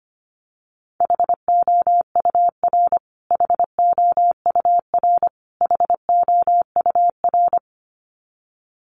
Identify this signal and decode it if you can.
Morse code, heard as a keyed tone, about 25 wpm, 700 Hz. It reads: HOUR HOUR HOUR
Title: Top 500 Words Repeated 25wpm